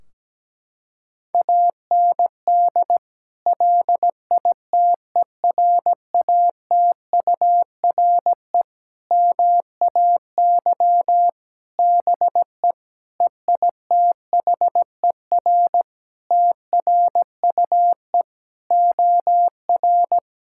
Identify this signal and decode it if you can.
Morse code, heard as a keyed tone, about 17 wpm, 700 Hz. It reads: AND LITERATURE MAY BE EITHER TRUE OR
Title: Sentences from Top 1600 Words 17wpm